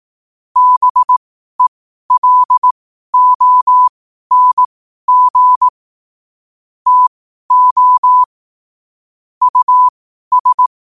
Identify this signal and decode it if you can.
Morse code, heard as a keyed tone, about 18 wpm, spaced as for 12 wpm, 1000 Hz. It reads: BELONG TO US
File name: qotdfiles / 12wpm